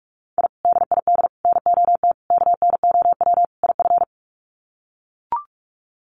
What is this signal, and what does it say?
Morse code, heard as a keyed tone, about 44 wpm, 700 Hz. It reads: I DID NOT KNOW IF
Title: Sets of 5 Words Encore 45wpm